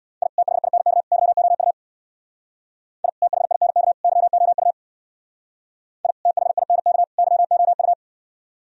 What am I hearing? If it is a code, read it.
Morse code, heard as a keyed tone, about 61 wpm, 700 Hz. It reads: I THINK YOU I THINK YOU I THINK YOU
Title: Speed-Racing - Sets of 3 Words 50wpm